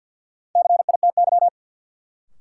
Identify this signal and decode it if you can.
Morse code, heard as a keyed tone, about 53 wpm, 700 Hz. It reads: CITY
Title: Sentences from Top 300 Words - New Word Drill - 50wpm